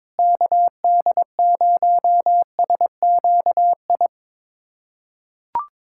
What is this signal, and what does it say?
Morse code, heard as a keyed tone, about 22 wpm, 700 Hz. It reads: KD0SQI
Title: USA Callsign DE Callsign 22wpm